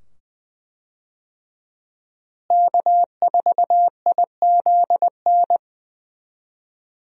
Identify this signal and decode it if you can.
Morse code, heard as a keyed tone, about 20 wpm, 700 Hz. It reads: K4IZN
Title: USA Call Signs - 5 Characters - Repeated 2x - 20wpm